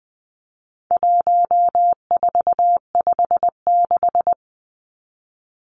Morse code, heard as a keyed tone, about 20 wpm, 700 Hz. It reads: 1456
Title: Numbers - 4-Digit 20wpm